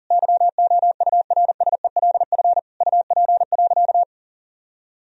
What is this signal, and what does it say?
Morse code, heard as a keyed tone, about 40 wpm, 700 Hz. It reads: YOURSELF UP.
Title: Sentences from Top 1000 Words - Review Part 1 - 40wpm